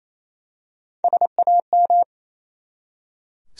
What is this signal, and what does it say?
Morse code, heard as a keyed tone, about 27 wpm, 700 Hz. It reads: SAM
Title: USA Common Names - Max 4 Letters - 28wpm